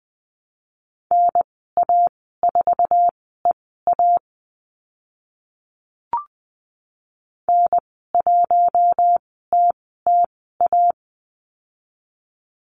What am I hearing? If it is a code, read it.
Morse code, heard as a keyed tone, about 20 wpm, spaced as for 14 wpm, 700 Hz. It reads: NA4EA N1TTA
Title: 076 Lesson 13g - Letter L - Call Signs 20wpm